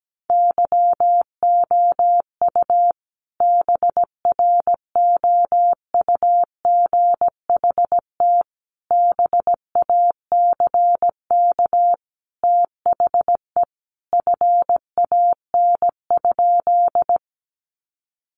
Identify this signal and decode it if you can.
Morse code, heard as a keyed tone, about 17 wpm, 700 Hz. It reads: YOU BROUGHT BACK THE FAN?
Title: Sentences from Top 1300 Words 17wpm